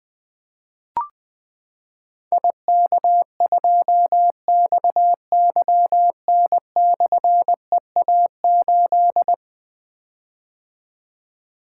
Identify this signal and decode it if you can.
Morse code, heard as a keyed tone, about 20 wpm, 700 Hz. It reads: IK2XYN/EA8
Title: Difficult Call Signs 20wpm